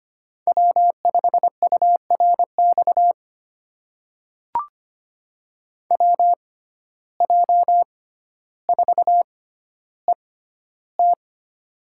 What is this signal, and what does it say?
Morse code, heard as a keyed tone, about 25 wpm, spaced as for 13 wpm, 700 Hz. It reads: W5URX WJ4ET
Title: USA Call Signs - 5 Characters - 6x Character Spacing - 25wpm